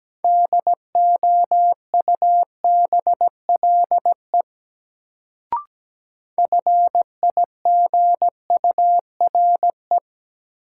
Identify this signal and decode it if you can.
Morse code, heard as a keyed tone, about 17 wpm, 700 Hz. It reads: DOUBLE FIGURE
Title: Sentences from Top 1400 Words - New Word Drill Alt - 17wpm